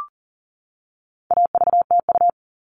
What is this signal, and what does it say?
Morse code, heard as a keyed tone, about 38 wpm, 700 Hz. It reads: AVTU
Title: All Cut Numbers - 4-Digit 40wpm